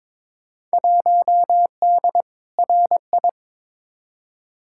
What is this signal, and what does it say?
Morse code, heard as a keyed tone, about 22 wpm, 700 Hz. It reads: 1D RI
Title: Field Day Exchanges - 22wpm